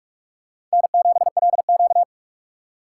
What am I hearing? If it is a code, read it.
Morse code, heard as a keyed tone, about 43 wpm, 700 Hz. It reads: N7LQ
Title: USA Call Signs - 4 Characters - 6x Character Spacing - 45wpm